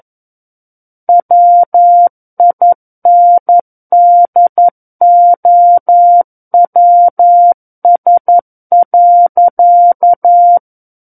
Morse code, heard as a keyed tone, about 11 wpm, 700 Hz. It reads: WINDOWS.
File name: Book One / 02-Chapter 2 - 11 WPM